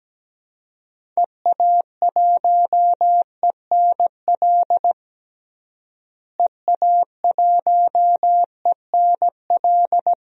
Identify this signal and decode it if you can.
Morse code, heard as a keyed tone, about 17 wpm, 700 Hz. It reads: EA1ENL EA1ENL
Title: DXCC-100 Call Signs - 6 Characters - Repeated 3x - 17wpm